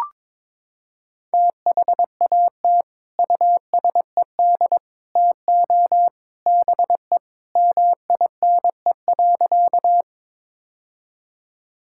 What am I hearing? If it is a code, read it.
Morse code, heard as a keyed tone, about 22 wpm, 700 Hz. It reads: THAT USED TO BE MINE.
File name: practice / 22wpm / Sentences with 3-Word Phrases - 22wpm